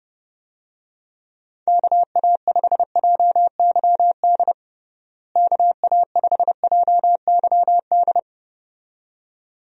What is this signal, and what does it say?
Morse code, heard as a keyed tone, about 31 wpm, 700 Hz. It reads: KA5JYD KA5JYD
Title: USA Call Signs - 6 Characters - Repeated 2x - 30wpm